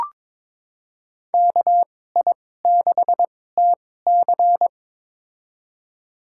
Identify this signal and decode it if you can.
Morse code, heard as a keyed tone, about 22 wpm, spaced as for 16 wpm, 700 Hz. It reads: KI6TC
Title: USA Call Signs - 5 Characters - 2x Character Spacing - 22wpm